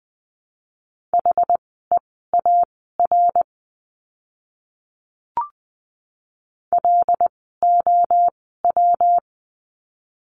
Morse code, heard as a keyed tone, about 20 wpm, spaced as for 14 wpm, 700 Hz. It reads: HEAR LOW
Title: 115 Lesson 19f - Letter w - Words 20wpm